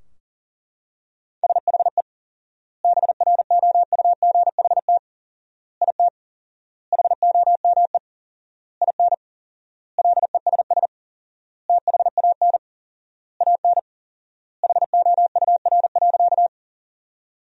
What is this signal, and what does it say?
Morse code, heard as a keyed tone, about 40 wpm, 700 Hz. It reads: SHE BROUGHT IT HOME IN LESS THAN AN HOUR.
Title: 4x Word-Spacing - Sentences from Top 500 Words - 40wpm